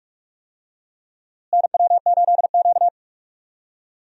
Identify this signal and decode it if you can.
Morse code, heard as a keyed tone, about 42 wpm, 700 Hz. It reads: NW8Q
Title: USA Call Signs - 4 Characters - Repeated 3x - 45wpm